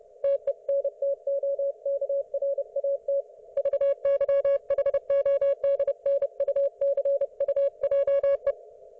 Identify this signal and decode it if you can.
Morse code, heard as a keyed tone, about 30 wpm, 560 Hz. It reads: TENTOKRAT VYHODNUCUJE
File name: Archiv / CUC208 / ok1fhi cul208